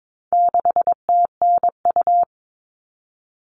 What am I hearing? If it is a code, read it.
Morse code, heard as a keyed tone, about 22 wpm, 700 Hz. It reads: 6TNU
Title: All Cut Numbers - 4-Digit 22wpm